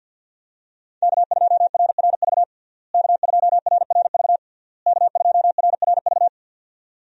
Morse code, heard as a keyed tone, about 46 wpm, 700 Hz. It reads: K2RRV K2RRV K2RRV
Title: USA Call Signs - 5 Characters - Repeated 3x - 50wpm